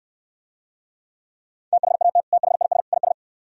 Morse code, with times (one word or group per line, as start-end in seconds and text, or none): 1.7-2.2 THAT
2.3-2.8 THIS
2.9-3.1 IS